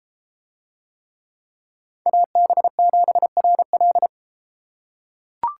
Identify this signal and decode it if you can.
Morse code, heard as a keyed tone, about 33 wpm, 700 Hz. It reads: AB7RL